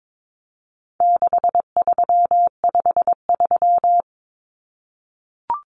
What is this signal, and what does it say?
Morse code, heard as a keyed tone, about 22 wpm, 700 Hz. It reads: 6353
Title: Numbers - 4-Digit 22wpm